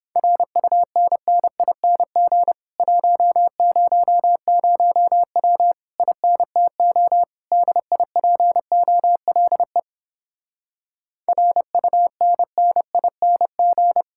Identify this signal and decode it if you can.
Morse code, heard as a keyed tone, about 31 wpm, 700 Hz. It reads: RUNNING 100W INTO DIPOLE RUNNING